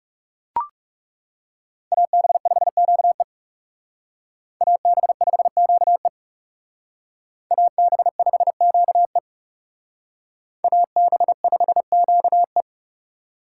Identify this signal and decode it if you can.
Morse code, heard as a keyed tone, about 36 wpm, 700 Hz. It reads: AB5QE AB5QE AB5QE AB5QE
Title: Speed Racing - USA 2x2 Call Signs 30wpm